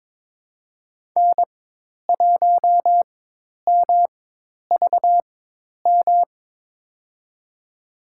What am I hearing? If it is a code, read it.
Morse code, heard as a keyed tone, about 22 wpm, spaced as for 10 wpm, 700 Hz. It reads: N1MVM
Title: USA Call Signs - 5 Characters - 4x Character Spacing - 22wpm